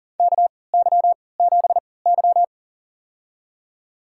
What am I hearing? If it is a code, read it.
Morse code, heard as a keyed tone, about 40 wpm, spaced as for 23 wpm, 700 Hz. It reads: KY7Y
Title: USA Call Signs - 3x Character Spacing - 40wpm